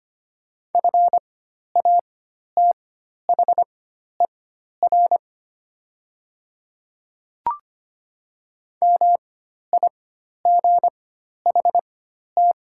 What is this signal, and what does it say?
Morse code, heard as a keyed tone, about 25 wpm, spaced as for 12 wpm, 700 Hz. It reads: FATHER MIGHT
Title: 4x Character Spacing - Top 300 Words - 25wpm